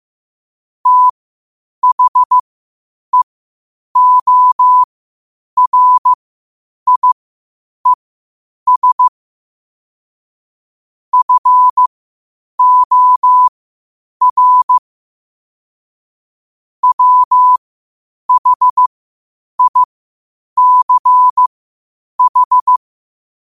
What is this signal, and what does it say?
Morse code, heard as a keyed tone, about 15 wpm, spaced as for 8 wpm, 1000 Hz. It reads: THEORIES FOR WHICH
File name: 8wpm